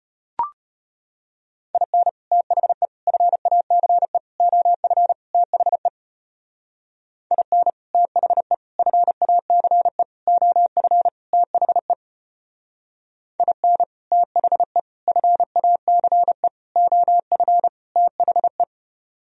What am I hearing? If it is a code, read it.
Morse code, heard as a keyed tone, about 31 wpm, 700 Hz. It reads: IN THE FACE OF THE IN THE FACE OF THE IN THE FACE OF THE
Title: Speed-Racing - Sets of 5 Words 25wpm